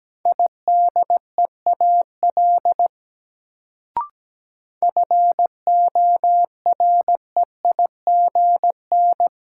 Morse code, heard as a keyed tone, about 17 wpm, 700 Hz. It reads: IDEAL FOREIGN
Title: Sentences from Top 1400 Words - New Word Drill - 17wpm